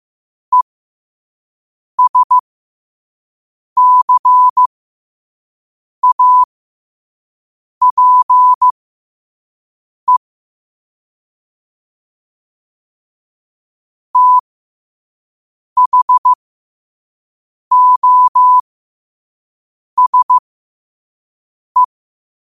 Morse code, heard as a keyed tone, about 15 wpm, spaced as for 5 wpm, 1000 Hz. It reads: ESCAPE THOSE
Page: New quotes every day in morse code at 5 Words per minute.